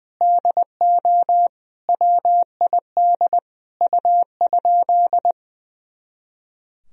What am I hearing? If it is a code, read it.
Morse code, heard as a keyed tone, about 21 wpm, 700 Hz. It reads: DO WID U?
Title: Sentences from Top 100 Words and Abbreviations 20wpm